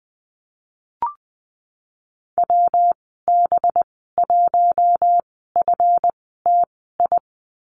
Morse code, heard as a keyed tone, about 20 wpm, spaced as for 14 wpm, 700 Hz. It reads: WB1FTI
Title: 193 Lesson 31g - Letter v - Call Signs 20wpm